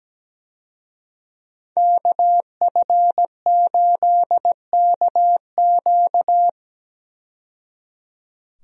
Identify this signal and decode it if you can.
Morse code, heard as a keyed tone, about 17 wpm, 700 Hz. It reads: KF8KQ
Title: USA Call Signs - 5 Characters - 17wpm